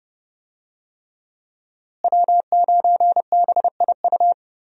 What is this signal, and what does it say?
Morse code, heard as a keyed tone, about 30 wpm, 700 Hz. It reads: W9BIU
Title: USA Call Signs - 5 Characters - Repeated 3x - 30wpm